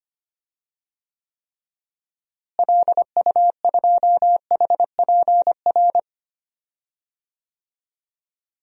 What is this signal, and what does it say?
Morse code, heard as a keyed tone, about 25 wpm, 700 Hz. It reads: LU2HPR
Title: DXCC-100 Call Signs - 6 Characters - Repeated 2x - 25wpm